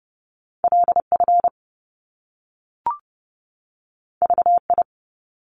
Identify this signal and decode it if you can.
Morse code, heard as a keyed tone, about 30 wpm, 700 Hz. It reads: LF VI
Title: Top Two Letter Combinations in Words 30wpm